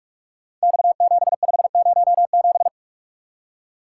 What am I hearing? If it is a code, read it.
Morse code, heard as a keyed tone, about 44 wpm, 700 Hz. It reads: X7507
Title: Code Groups - 5 Letter:Numbers - 45wpm